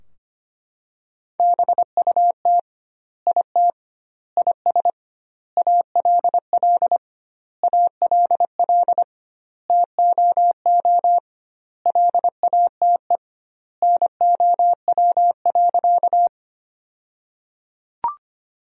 Morse code, heard as a keyed tone, about 25 wpm, 700 Hz. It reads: BUT IT IS ALL ALL TOO LATE NOW.
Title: 2x Word-Spacing - Sentences from Top 300 Words - 25wpm